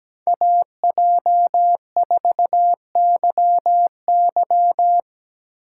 Morse code, heard as a keyed tone, about 17 wpm, 700 Hz. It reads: AJ4YY
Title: USA Call Signs - 5 Characters - 3x Character Spacing - 17wpm